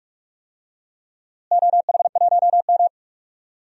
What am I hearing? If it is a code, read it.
Morse code, heard as a keyed tone, about 41 wpm, 700 Hz. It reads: OH1M
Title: DXCC-100 Call Signs - 4 Characters - Repeated 3x - 45wpm